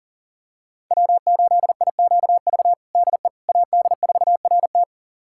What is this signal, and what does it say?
Morse code, heard as a keyed tone, about 40 wpm, 700 Hz. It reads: W8IQV DE AD4RT
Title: USA Callsign DE Callsign 40wpm